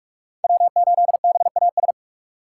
Morse code, heard as a keyed tone, about 47 wpm, 700 Hz. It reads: W8BAS
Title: USA Call Signs - 2x Character Spacing - 45wpm